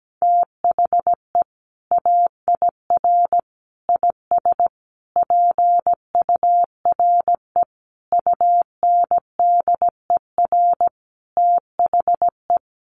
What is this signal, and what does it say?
Morse code, heard as a keyed tone, about 17 wpm, 700 Hz. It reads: THE AIR IS PURE UNDER THE
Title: Sentences from Top 1500 Words 17wpm